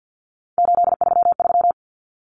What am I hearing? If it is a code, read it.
Morse code, heard as a keyed tone, about 50 wpm, 700 Hz. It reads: 833
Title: Common Cut Numbers - 3-Digit 50wpm